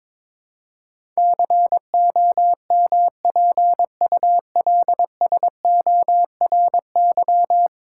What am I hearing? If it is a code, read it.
Morse code, heard as a keyed tone, about 22 wpm, 700 Hz. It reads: COMPULSORY
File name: practice / 22wpm / 10 Letter Words - First 3 Letters Given - 22wpm